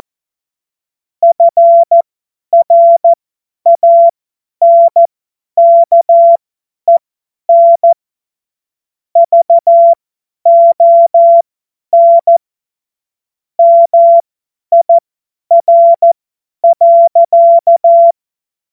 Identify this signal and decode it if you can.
Morse code, heard as a keyed tone, about 14 wpm, spaced as for 10 wpm, 680 Hz. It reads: FRANKEN VON MIR.
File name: Morsekurs / Witze / Witz 03. Mai 2022 50 F70